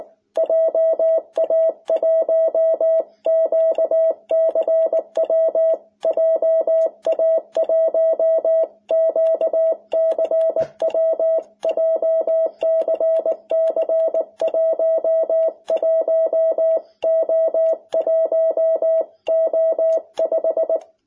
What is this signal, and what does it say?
Morse code, heard as a keyed tone, about 19 wpm, 660 Hz. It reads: JA1QCW JA1QCW JCC 11O1O5
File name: jh8cbh / 2021cw